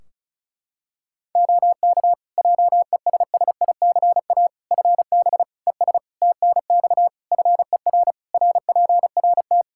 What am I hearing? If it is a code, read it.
Morse code, heard as a keyed tone, about 36 wpm, 700 Hz. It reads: OK JESSICA FB ES TNX FER RPRT
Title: Practice QSO - Exchange 2 - Info - 35wpm